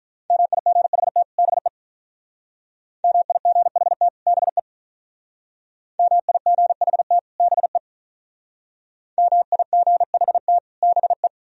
Morse code, heard as a keyed tone, about 47 wpm, 700 Hz. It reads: MIGHT BE MIGHT BE MIGHT BE MIGHT BE